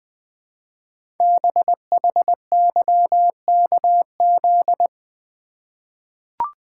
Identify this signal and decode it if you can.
Morse code, heard as a keyed tone, about 20 wpm, 700 Hz. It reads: BHYKZ
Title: Code Groups - 5 Letters - 20wpm